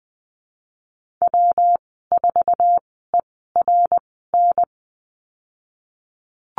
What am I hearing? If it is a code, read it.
Morse code, heard as a keyed tone, about 20 wpm, spaced as for 15 wpm, 700 Hz. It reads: W4ERN
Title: 140 Lesson 23g - Letter f - Call Signs 20wpm